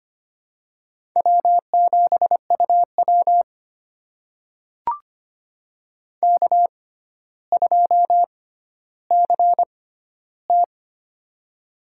Morse code, heard as a keyed tone, about 25 wpm, spaced as for 13 wpm, 700 Hz. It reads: W7UW K2CT
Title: USA Call Signs - 4 Characters - 6x Character Spacing - 25wpm